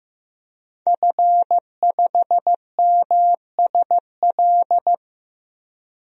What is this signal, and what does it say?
Morse code, heard as a keyed tone, about 15 wpm, 700 Hz. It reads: F5MSL
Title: DXCC-100 Call Signs - 5 Characters - 15wpm